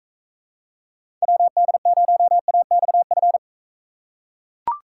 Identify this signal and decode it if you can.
Morse code, heard as a keyed tone, about 42 wpm, 700 Hz. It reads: WD0AXF